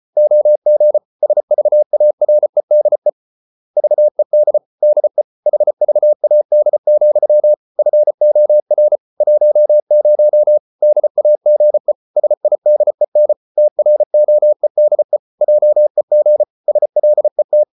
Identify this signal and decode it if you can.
Morse code, heard as a keyed tone, about 34 wpm, 600 Hz. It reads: OG SVAREDE VED DE HVAD, FOR 10 DAGE SIDEN TROEDE JEG SLET
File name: Ferie QRQ 34wpm